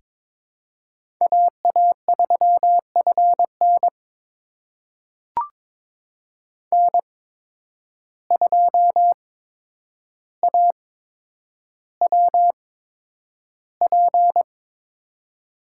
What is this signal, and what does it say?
Morse code, heard as a keyed tone, about 21 wpm, spaced as for 9 wpm, 700 Hz. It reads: AA3FN N2AWP
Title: USA Call Signs - 5 Characters - 8x Character Spacing - 22wpm